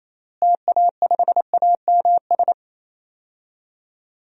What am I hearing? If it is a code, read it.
Morse code, heard as a keyed tone, about 28 wpm, 700 Hz. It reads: TA5AMS
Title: DXCC-100 Call Signs - 6 Characters - Repeated 3x - 28wpm